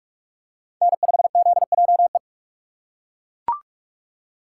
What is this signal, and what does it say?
Morse code, heard as a keyed tone, about 41 wpm, 700 Hz. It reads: N5ZJE